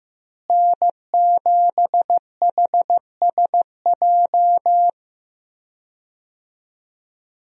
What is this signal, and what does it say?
Morse code, heard as a keyed tone, about 15 wpm, 700 Hz. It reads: N7HSJ
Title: USA Call Signs - 5 Characters - Repeated 3x - 15wpm